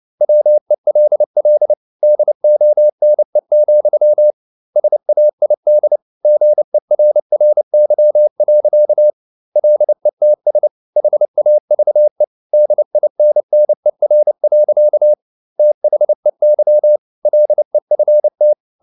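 Morse code, heard as a keyed tone, about 29 wpm, 600 Hz. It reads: WELL DONE, SAID GERRY. LETS HAVE DINNER. THEY LEFT
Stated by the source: Morse code at 29 wpm, sending Ken Follett's novel Never in English, from CW med Gnister.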